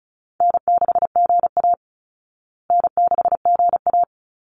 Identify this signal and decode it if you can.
Morse code, heard as a keyed tone, about 34 wpm, 700 Hz. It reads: N6GA N6GA
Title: Easy Call Signs Repeated 3x - 35wpm